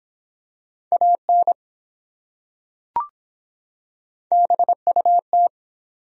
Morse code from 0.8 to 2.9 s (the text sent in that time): AN